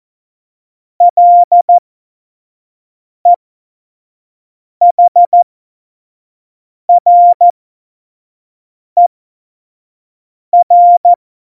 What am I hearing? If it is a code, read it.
Morse code, heard as a keyed tone, about 14 wpm, spaced as for 5 wpm, 700 Hz. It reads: LEHRER
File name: Morsekurs / Witze / Witz 21. Mai 2019, 25 F70